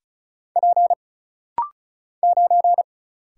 Morse code, heard as a keyed tone, about 36 wpm, 700 Hz. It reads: P 9
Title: Single Letter-Number Alt - Rapid-Fire 35wpm